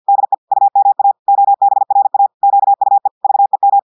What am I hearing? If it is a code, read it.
Morse code, heard as a keyed tone, about 50 wpm, 800 Hz. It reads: DE UMA OBRA QUE VEM
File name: audios / portugais / portugais texte en clair 1 vitesse 50 mots minutes